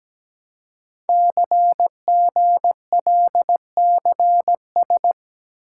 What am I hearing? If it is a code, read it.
Morse code, heard as a keyed tone, about 17 wpm, 700 Hz. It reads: CGLCS
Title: Code Groups - 5 Letters - 17wpm